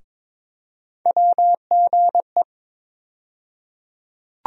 Morse code, heard as a keyed tone, about 22 wpm, 700 Hz. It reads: WGE
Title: USA Call Sign Postfixes - 3 Letters 22wpm